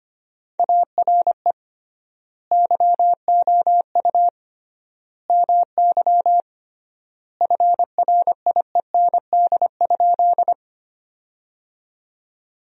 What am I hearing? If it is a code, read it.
Morse code, heard as a keyed tone, about 25 wpm, 700 Hz. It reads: ARE YOU MY FRIEND?
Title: 3x Word-Spacing - Sentences from Top 300 Words - 25wpm